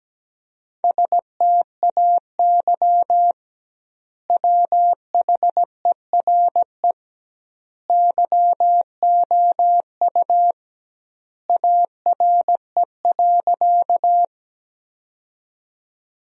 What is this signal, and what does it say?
Morse code, heard as a keyed tone, about 17 wpm, 700 Hz. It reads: STAY WHERE YOU ARE.
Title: 2x Word-Spacing - Sentences from Top 500 Words - 17wpm